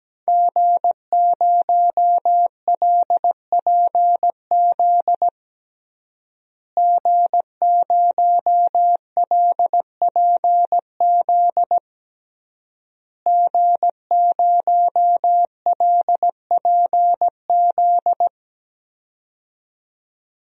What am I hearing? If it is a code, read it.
Morse code, heard as a keyed tone, about 17 wpm, 700 Hz. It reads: G0LPZ G0LPZ G0LPZ
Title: DXCC-100 Call Signs - 5 Characters - Repeated 3x - 17wpm